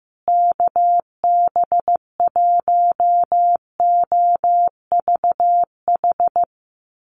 Morse code, heard as a keyed tone, about 15 wpm, 700 Hz. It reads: KB1OVH
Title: USA Call Signs Repeated 2x 15wpm